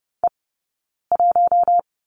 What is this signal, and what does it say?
Morse code, heard as a keyed tone, about 30 wpm, 700 Hz. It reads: E 1
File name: Lesson 19d - Letter w - Two Characters 30wpm